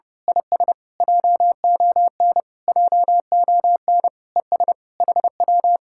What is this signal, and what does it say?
Morse code, heard as a keyed tone, about 28 wpm, 700 Hz. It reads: IS JON JON ES HW
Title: Speed-Racing Common QSO Phrases Encore - 30wpm